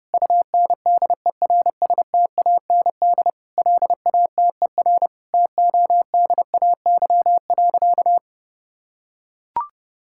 Morse code, heard as a keyed tone, about 32 wpm, 700 Hz. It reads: UNDERSTAND LATER TODAY.
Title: Sentences from Top 1000 Words - Review Part 5 - 30wpm